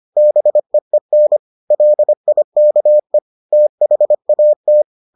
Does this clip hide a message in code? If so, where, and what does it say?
Yes: Morse code from 0.2 to 4.8 s, heard as a keyed tone, about 25 wpm, 600 Hz. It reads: BEEN LIKE THAT